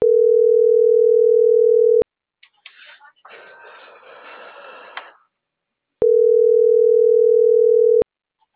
Ringback tone (North America), 440 + 480 Hz.
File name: CENTRAL MABE / out edwinomarojeda